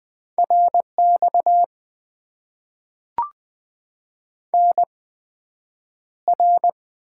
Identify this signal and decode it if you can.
Morse code, heard as a keyed tone, about 20 wpm, spaced as for 9 wpm, 700 Hz. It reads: RX NR
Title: Top 90 QSO Elements - 8x Character Spacing - 20wpm